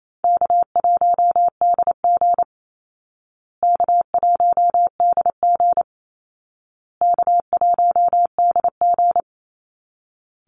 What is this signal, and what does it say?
Morse code, heard as a keyed tone, about 28 wpm, 700 Hz. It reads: K1DG K1DG K1DG
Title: Easy Call Signs Repeated 3x - 28wpm